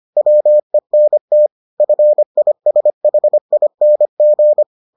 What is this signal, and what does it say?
Morse code, heard as a keyed tone, about 25 wpm, 600 Hz. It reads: WENT FISHING